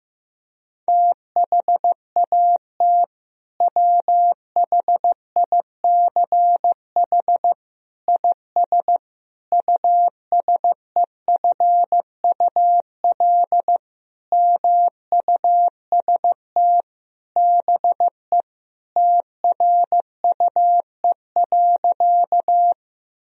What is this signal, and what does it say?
Morse code, heard as a keyed tone, about 15 wpm, 700 Hz. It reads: THAT WHICH IS USEFUL MUST BE TRUE.
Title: Sentences from Top 1000 Words - Review Part 4 - 15wpm